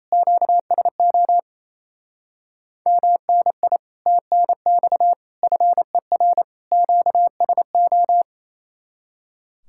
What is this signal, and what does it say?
Morse code, heard as a keyed tone, about 36 wpm, 700 Hz. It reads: QSO MNI TNX FER QSO